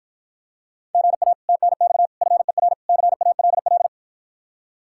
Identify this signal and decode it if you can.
Morse code, heard as a keyed tone, about 52 wpm, 700 Hz. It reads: GA TNX FER CALL